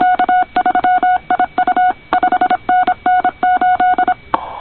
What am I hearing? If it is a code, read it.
Morse code, heard as a keyed tone, about 26 wpm, 1470 Hz, with noise. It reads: K3IU 5NN 8
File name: co8tw / cqwwcw2010